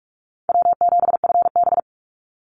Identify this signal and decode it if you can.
Morse code, heard as a keyed tone, about 44 wpm, 700 Hz. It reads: W7FB
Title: Call Signs Repeated 3x - 45wpm